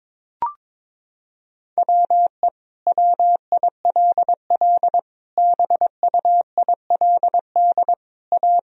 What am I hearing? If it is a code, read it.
Morse code, heard as a keyed tone, about 22 wpm, 700 Hz. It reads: WE WILL BUILD A
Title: Sentences from Top 1000 Words - Review Part 4 - 22wpm